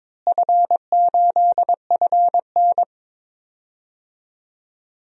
Morse code, heard as a keyed tone, about 22 wpm, 700 Hz. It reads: F8FN
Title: DXCC-100 Call Signs - 4 Characters - Repeated 2x - 22wpm